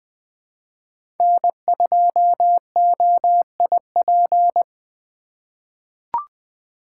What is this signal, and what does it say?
Morse code, heard as a keyed tone, about 20 wpm, 700 Hz. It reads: N2OIP
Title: USA Call Signs - 5 Characters - 3x Character Spacing - 20wpm